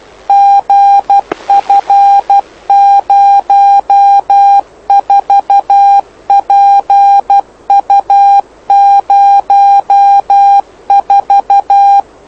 Morse code, heard as a keyed tone, about 12 wpm, 800 Hz. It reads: GF04PU04
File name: lu4aao / lu7aa 09dic2011 exp planeador vuelo1